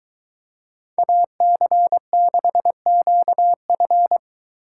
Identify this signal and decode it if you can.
Morse code, heard as a keyed tone, about 22 wpm, 700 Hz. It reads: AC6QF